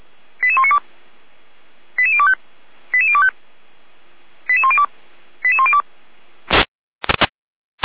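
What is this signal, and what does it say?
Five-tone selective call (ZVEI), heard as quick runs of five tones, tones of about 70 ms. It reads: 88192 88025 88025 88102 88102
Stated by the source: Received by SDR; ZVEI five-tone selective call.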